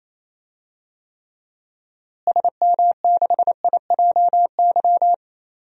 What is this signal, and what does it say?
Morse code, heard as a keyed tone, about 28 wpm, 700 Hz. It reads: SM6IJY
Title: DXCC-100 Call Signs - 6 Characters - Repeated 2x - 28wpm